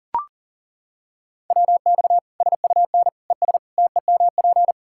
Morse code, heard as a keyed tone, about 40 wpm, 700 Hz. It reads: WX SUN ES TEMP
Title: K1USN SST QSO Practice 40wpm